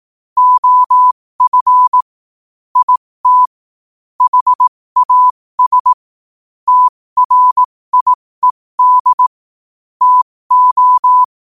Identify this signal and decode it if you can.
Morse code, heard as a keyed tone, about 18 wpm, spaced as for 15 wpm, 1000 Hz. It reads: OF IT HAS TRIED TO
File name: qotdfiles / 15wpm